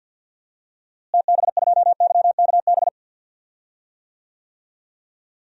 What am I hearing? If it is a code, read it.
Morse code, heard as a keyed tone, about 47 wpm, 700 Hz. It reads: TB2YKB
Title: DXCC-100 Call Signs - 6 Characters - Repeated 2x - 50wpm